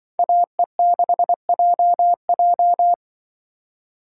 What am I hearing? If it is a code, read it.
Morse code, heard as a keyed tone, about 24 wpm, 700 Hz. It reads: AE6JJ
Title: Speed Racing - USA 2x2 Call Signs 20wpm